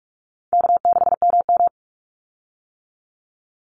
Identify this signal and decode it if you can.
Morse code, heard as a keyed tone, about 44 wpm, 700 Hz. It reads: K6MM
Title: Easy Call Signs - 45wpm